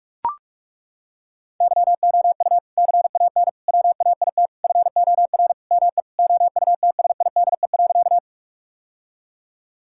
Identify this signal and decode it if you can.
Morse code, heard as a keyed tone, about 41 wpm, 700 Hz. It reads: YOU CAN WAIT FOR ME OUTSIDE.
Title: Sentences from Top 1000 Words - Review Part 6 - 45wpm